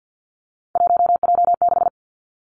Morse code, heard as a keyed tone, about 50 wpm, 700 Hz. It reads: 1J6
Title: Three Challenging Letter-Number Combinations 50wpm